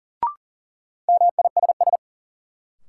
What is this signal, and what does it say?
Morse code, heard as a keyed tone, about 40 wpm, 700 Hz. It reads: MISS
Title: Sentences from Top 500 Words - New Word Drill - 40wpm